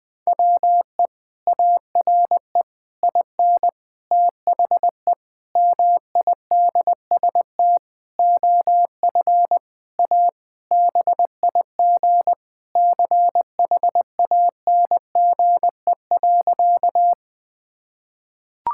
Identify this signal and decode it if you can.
Morse code, heard as a keyed tone, about 21 wpm, 700 Hz. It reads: WE ARE IN THE MIDST OF A BIG CHANGE.
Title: Sentences with 7-Word Phrases - 20wpm